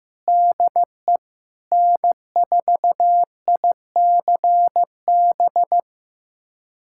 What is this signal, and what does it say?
Morse code, heard as a keyed tone, about 15 wpm, 700 Hz. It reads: DE N4ICB
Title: USA Callsign DE Callsign 15wpm